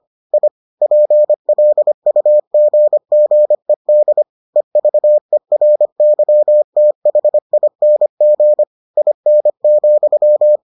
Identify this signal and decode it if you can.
Morse code, heard as a keyed tone, about 25 wpm, 600 Hz. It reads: I PLUGGED EVERYTHING IN,